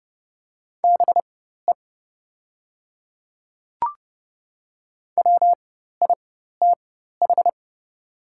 Morse code, heard as a keyed tone, about 30 wpm, spaced as for 14 wpm, 700 Hz. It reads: BE WITH
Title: 4x Character Spacing - Top 200 Words - 30wpm